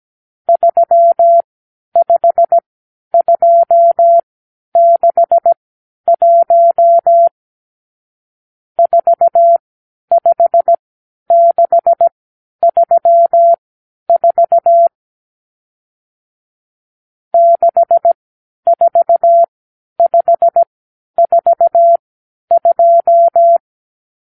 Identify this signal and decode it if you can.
Morse code, heard as a keyed tone, about 17 wpm, spaced as for 10 wpm, 680 Hz. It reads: 35261 45634 64542
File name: k7qo / Lesson 50 (1-6)